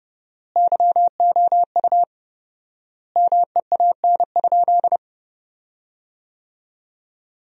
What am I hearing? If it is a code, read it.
Morse code, heard as a keyed tone, about 31 wpm, 700 Hz. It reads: YOU MEAN?
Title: 4x Word-Spacing - Sentences from Top 200 Words - 30wpm